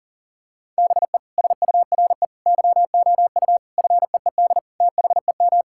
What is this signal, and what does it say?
Morse code, heard as a keyed tone, about 37 wpm, 700 Hz. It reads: BE SURE YOU FEED THEM
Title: Sentences from Top 1600 Words 40wpm